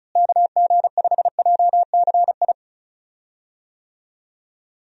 Morse code, heard as a keyed tone, about 34 wpm, 700 Hz. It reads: KG5JCI
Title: USA Call Signs - 6 Characters - Repeated 3x - 35wpm